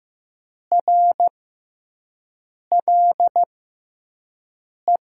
Morse code, heard as a keyed tone, about 15 wpm, spaced as for 5 wpm, 700 Hz. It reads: RLE
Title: USA Call Signs - 6 Characters - 6x Character Spacing - 15wpm